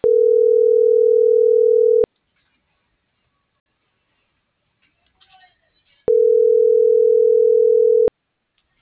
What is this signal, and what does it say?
Ringback tone (North America), 440 + 480 Hz.